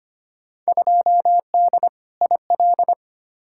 Morse code, heard as a keyed tone, about 25 wpm, 700 Hz. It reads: 2D IL
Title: Field Day Exchanges - 25wpm